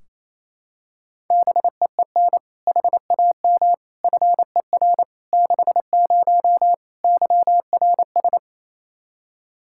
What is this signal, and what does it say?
Morse code, heard as a keyed tone, about 28 wpm, 700 Hz. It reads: BEEN HAM FER 60 YRS
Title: Practice QSO - Exchange 3 - More Info - 28wpm